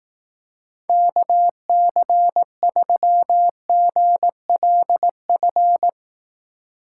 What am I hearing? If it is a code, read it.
Morse code, heard as a keyed tone, about 18 wpm, 700 Hz. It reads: KC3GLF
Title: Speed Racing - USA Call Signs 15wpm